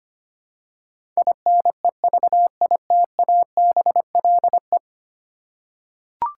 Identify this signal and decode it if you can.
Morse code, heard as a keyed tone, about 25 wpm, 700 Hz. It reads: INEVITABLE
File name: Ten Letter Words - 25wpm